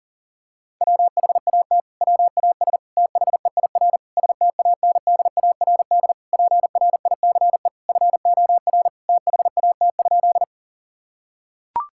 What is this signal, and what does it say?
Morse code, heard as a keyed tone, about 40 wpm, 700 Hz. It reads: WHAT WAS THEIR STANDARD PRICE FOR THAT?
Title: Sentences from Top 1000 Words - Review Part 6 - 40wpm